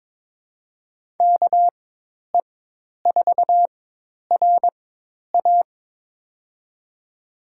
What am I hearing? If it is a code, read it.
Morse code, heard as a keyed tone, about 22 wpm, spaced as for 10 wpm, 700 Hz. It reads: KE4RA
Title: USA Call Signs - 4x Character Spacing - 22wpm